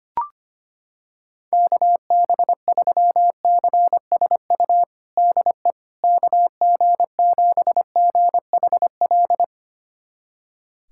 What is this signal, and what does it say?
Morse code, heard as a keyed tone, about 25 wpm, 700 Hz. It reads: KB3CSU DE KG7GHL
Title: USA Callsign DE Callsign - 6 Characters - 25wpm